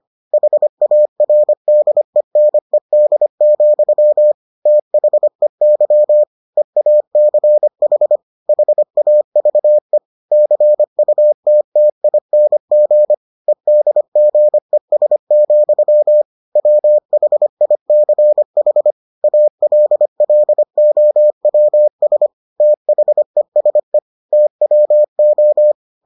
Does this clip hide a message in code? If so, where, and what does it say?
Yes: Morse code from 0.3 to 25.7 s, heard as a keyed tone, about 25 wpm, 600 Hz. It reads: HARDENED, THEY EACH HAVE CUTTING EDGES, WHICH ALLOWS THESE TWO